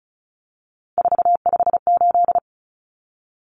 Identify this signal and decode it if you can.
Morse code, heard as a keyed tone, about 34 wpm, 700 Hz. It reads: 458
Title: Common Cut Numbers - 3-Digit 35wpm